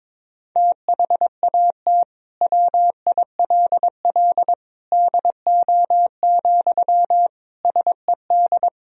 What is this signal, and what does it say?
Morse code, heard as a keyed tone, about 21 wpm, 700 Hz. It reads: THAT WILL DO, SED
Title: Sentences from Top 100 Words and Abbreviations 22wpm